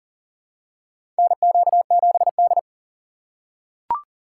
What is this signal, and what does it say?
Morse code, heard as a keyed tone, about 40 wpm, 700 Hz. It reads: NQ7D
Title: USA Call Signs - 4 Characters - 2x Character Spacing - 40wpm